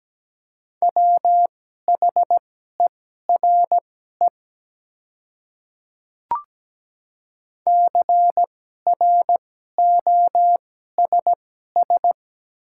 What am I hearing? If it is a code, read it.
Morse code, heard as a keyed tone, about 17 wpm, spaced as for 12 wpm, 700 Hz. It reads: WHERE CROSS
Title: 2x Character Spacing - Top 300 Words - 17wpm